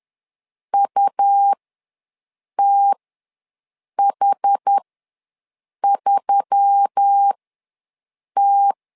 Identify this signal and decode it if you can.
Morse code, heard as a keyed tone, about 10 wpm, spaced as for 6 wpm, 800 Hz. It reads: UTH3T
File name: LEZIONI / 4 Q1S3 ws 10wpm 2437